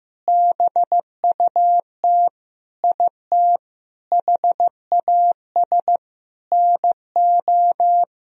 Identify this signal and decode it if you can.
Morse code, heard as a keyed tone, about 16 wpm, 700 Hz. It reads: BUT IT HAS NO
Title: Sentences from Top 1000 Words - Review Part 1 - 15wpm